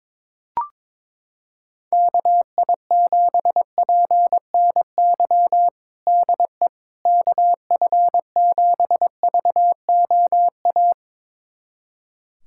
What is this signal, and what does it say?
Morse code, heard as a keyed tone, about 22 wpm, 700 Hz. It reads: KI7PNY DE KF7VOA
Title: USA Callsign DE Callsign - 6 Characters - 22wpm